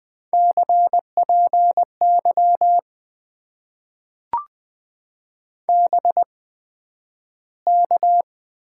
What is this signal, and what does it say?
Morse code, heard as a keyed tone, about 20 wpm, spaced as for 11 wpm, 700 Hz. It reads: CPY BK
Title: Top 90 QSO Elements - 8x Character Spacing - 20wpm